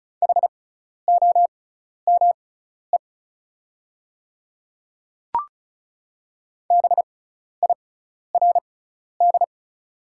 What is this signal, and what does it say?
Morse code, heard as a keyed tone, about 36 wpm, spaced as for 12 wpm, 700 Hz. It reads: HOME BIRD
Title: 6x Character Spacing - Top 500 Words - 35wpm